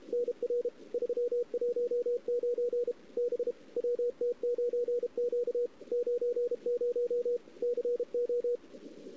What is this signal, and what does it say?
Morse code, heard as a keyed tone, about 32 wpm, 480 Hz, with radio noise. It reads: NR 319 B WT9Q 90 CO
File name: CW / 2008-ss-cw-k5tr-op-n5rz -01:00